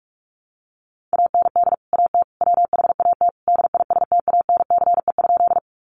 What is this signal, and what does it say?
Morse code, heard as a keyed tone, about 44 wpm, 700 Hz. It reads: AND AT WHAT DISTANCE?
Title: Sentences from Top 1200 Words 45wpm